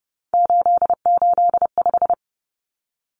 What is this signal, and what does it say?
Morse code, heard as a keyed tone, about 30 wpm, 700 Hz. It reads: 885
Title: Numbers - 3-Digit 30wpm